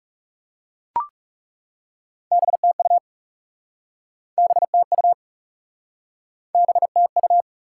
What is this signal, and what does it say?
Morse code, heard as a keyed tone, about 40 wpm, 700 Hz. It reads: BTU BTU BTU